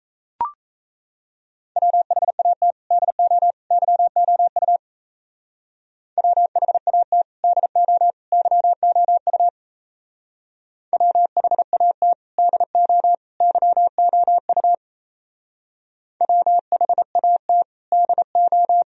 Morse code, heard as a keyed tone, about 33 wpm, 700 Hz. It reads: WHAT DO YOU WHAT DO YOU WHAT DO YOU WHAT DO
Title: Speed-Racing - Sets of 3 Words 28wpm